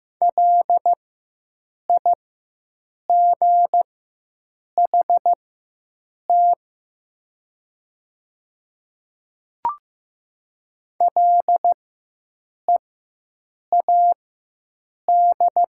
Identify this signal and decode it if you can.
Morse code, heard as a keyed tone, about 15 wpm, spaced as for 7 wpm, 700 Hz. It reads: LIGHT LEAD
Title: 4x Character Spacing - Top 500 Words - 15wpm